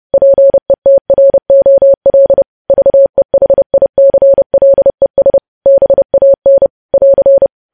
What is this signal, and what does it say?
Morse code, heard as a keyed tone, about 30 wpm, 550 Hz. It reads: PETROL VEHICLES BAN <AR>
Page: news headlines in morse code at 30 wpm